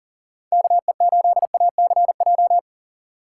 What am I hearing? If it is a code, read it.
Morse code, heard as a keyed tone, about 40 wpm, 700 Hz. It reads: KE8ACJ
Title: USA Call Signs - 6 Characters - 4x Character Spacing - 40wpm